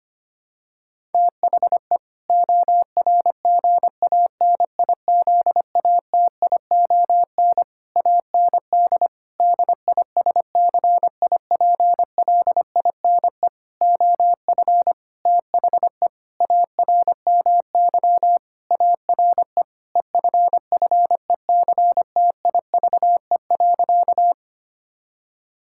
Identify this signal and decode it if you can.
Morse code, heard as a keyed tone, about 25 wpm, 700 Hz. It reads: THE ORGANIZATION AND DISCIPLINE OF THE ARMY ARE EFFECTIVE.
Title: Sentences from Top 1000 Words - Review Part 6 - 25wpm